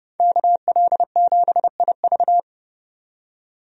Morse code, heard as a keyed tone, about 30 wpm, 700 Hz. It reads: KL7IV
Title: DXCC-100 Call Signs - 5 Characters - 30wpm